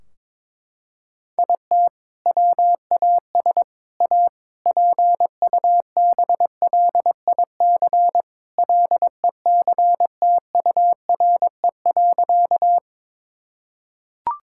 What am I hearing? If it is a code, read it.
Morse code, heard as a keyed tone, about 22 wpm, 700 Hz. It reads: IT WAS A PUBLIC LECTURE.
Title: Sentences from Top 1600 Words 22wpm